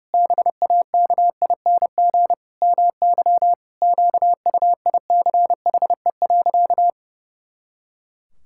Morse code, heard as a keyed tone, about 30 wpm, 700 Hz. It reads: BAKING MY QUICHE.
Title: Pangram Challenge v2 - 30wpm